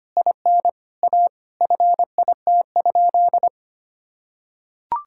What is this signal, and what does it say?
Morse code, heard as a keyed tone, about 25 wpm, 700 Hz. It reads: IN A FIT?
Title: Sentences from Top 1000 Words - Review Part 5 - 25wpm